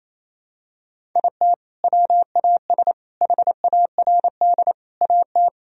Morse code, heard as a keyed tone, about 28 wpm, 700 Hz. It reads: IT WAS HARD AT
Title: Sentences from Top 1000 Words - Review Part 1 - 28wpm